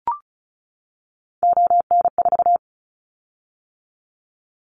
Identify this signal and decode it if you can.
Morse code, heard as a keyed tone, about 36 wpm, 700 Hz. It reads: ON4
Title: Callsign Prefixes for Top 25 DXCC - 35wpm